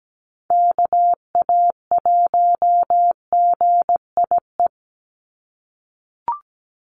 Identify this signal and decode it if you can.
Morse code, heard as a keyed tone, about 17 wpm, 700 Hz. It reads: KA1GIE
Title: USA Call Signs 17wpm